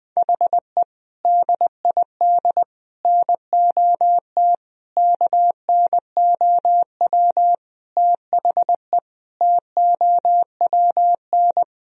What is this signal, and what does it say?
Morse code, heard as a keyed tone, about 20 wpm, 700 Hz. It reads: HE DID NOT KNOW THE TOWN
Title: Sentences from Top 1000 Words - Review Part 2 - 20wpm